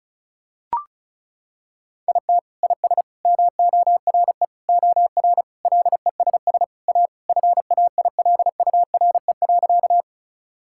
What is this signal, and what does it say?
Morse code, heard as a keyed tone, about 36 wpm, 700 Hz. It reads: IT IS MORE OR LESS A FAILURE.
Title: Sentences from Top 1000 Words - Review Part 6 - 35wpm